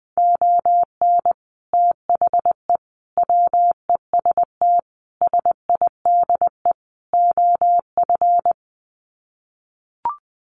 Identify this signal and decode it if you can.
Morse code, heard as a keyed tone, about 20 wpm, 700 Hz. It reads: ON THE WEST SIDE OF
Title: Sets of 5 Words Encore2 20wpm